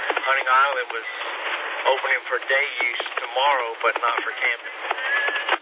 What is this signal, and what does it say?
Emergency Alert System (SAME) tones, heard as harsh buzzing bursts, with speech. It is heard from 0.8 to 4.7 s.